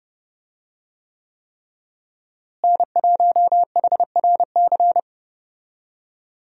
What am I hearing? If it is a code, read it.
Morse code, heard as a keyed tone, about 30 wpm, 700 Hz. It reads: N1HRC
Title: USA Call Signs - 5 Characters - Repeated 3x - 30wpm